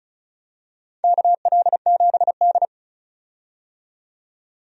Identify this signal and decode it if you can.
Morse code, heard as a keyed tone, about 34 wpm, 700 Hz. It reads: KL7D
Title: DXCC-100 Call Signs - 4 Characters - 35wpm